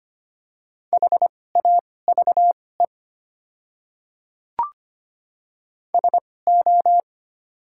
Morse code, heard as a keyed tone, about 25 wpm, spaced as for 18 wpm, 690 Hz. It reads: HAVE SO
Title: 2x Character Spacing - Top 100 Words - 25wpm